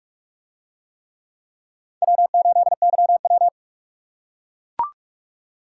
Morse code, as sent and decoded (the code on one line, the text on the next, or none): .-- ---.. -.-- .--
W8YW